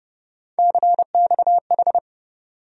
Morse code, heard as a keyed tone, about 30 wpm, 700 Hz. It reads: CXH
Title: USA Call Sign Postfixes - 3 Letters 30wpm